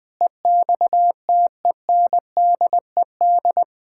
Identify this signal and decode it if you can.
Morse code, heard as a keyed tone, about 20 wpm, 700 Hz. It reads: EXTENDED
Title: Sentences with 6-Word Phrases - 20wpm